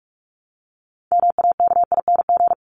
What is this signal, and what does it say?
Morse code, heard as a keyed tone, about 44 wpm, 700 Hz. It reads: MAKING
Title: ING Suffix - Top 100 Words with Base Words 45wpm